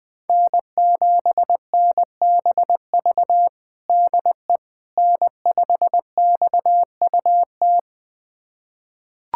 Morse code, heard as a keyed tone, about 20 wpm, 700 Hz. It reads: N7NBV DE N5XUT
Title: USA Callsign DE Callsign - 5 Characters - 20wpm